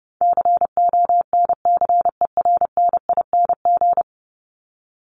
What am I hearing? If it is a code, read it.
Morse code, heard as a keyed tone, about 30 wpm, 700 Hz. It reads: CONCERNING
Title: ING Suffix - Top 100 Words with Base Words 30wpm